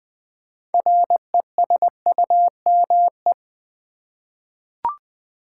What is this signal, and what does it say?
Morse code, heard as a keyed tone, about 20 wpm, 700 Hz. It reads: RESUME
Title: 6 Letter Words - First 3 Letters Given - 20wpm